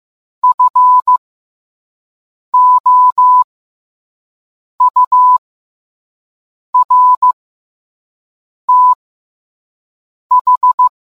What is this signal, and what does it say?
Morse code, heard as a keyed tone, about 15 wpm, spaced as for 5 wpm, 1000 Hz. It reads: FOURTH